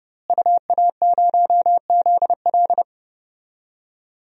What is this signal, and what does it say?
Morse code, heard as a keyed tone, about 32 wpm, 700 Hz. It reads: UA0ZL
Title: DXCC-100 Call Signs - 5 Characters - Repeated 3x - 30wpm